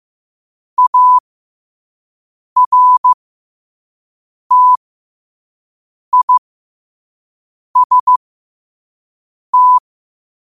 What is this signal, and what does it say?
Morse code, heard as a keyed tone, about 15 wpm, spaced as for 5 wpm, 1000 Hz. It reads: ARTIST
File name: qotdfiles / 5wpm